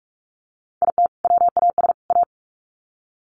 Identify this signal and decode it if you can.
Morse code, heard as a keyed tone, about 44 wpm, 700 Hz. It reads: IT WAS A
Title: Sets of 3 Words 45wpm